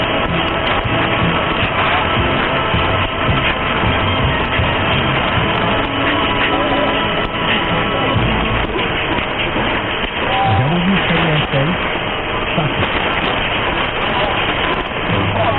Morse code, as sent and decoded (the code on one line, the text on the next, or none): none